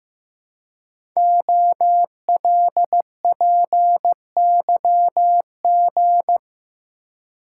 Morse code, heard as a keyed tone, about 15 wpm, 700 Hz. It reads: OLPYG
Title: Code Groups - 5 Letters - 15wpm